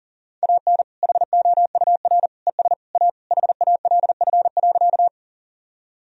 Morse code, heard as a keyed tone, about 37 wpm, 700 Hz. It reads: AN HOUR ES A HALF.
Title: Sentences from Top 400 Words and Abbreviations 40wpm